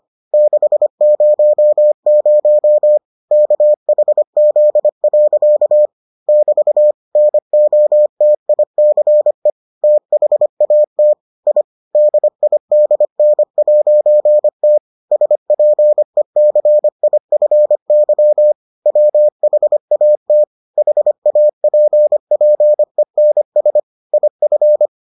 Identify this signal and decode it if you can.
Morse code, heard as a keyed tone, about 25 wpm, 600 Hz. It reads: 600 KHZ. <BT> NOTICE THAT I DIDN'T SPECIFY WHAT HAPPENS IF